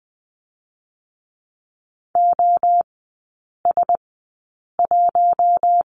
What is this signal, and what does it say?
Morse code, heard as a keyed tone, about 20 wpm, spaced as for 8 wpm, 700 Hz. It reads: OS1
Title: 060 Lesson 11e - Letter h - Three Characters 20wpm